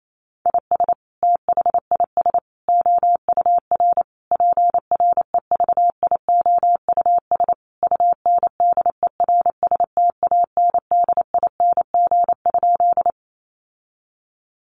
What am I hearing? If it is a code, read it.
Morse code, heard as a keyed tone, about 28 wpm, 700 Hz. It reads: IS THIS OUR PREVIOUS UNDERSTANDING?
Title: Sentences from Top 1500 Words 28wpm